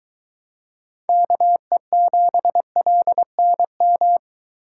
Morse code, heard as a keyed tone, about 22 wpm, 700 Hz. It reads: KE7LNM